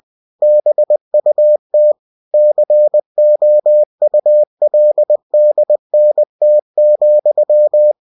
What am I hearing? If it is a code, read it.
Morse code, heard as a keyed tone, about 20 wpm, 600 Hz. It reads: BUT COULDNT,
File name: cw-books / winnie-the-pooh / winnie-20wpm-page-0102